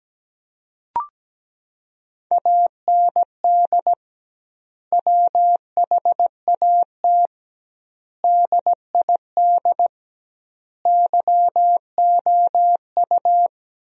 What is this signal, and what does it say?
Morse code, heard as a keyed tone, about 17 wpm, 700 Hz. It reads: AND WHAT DID YOU
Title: 2x Word-Spacing - Sentences from Top 100 Words - 17wpm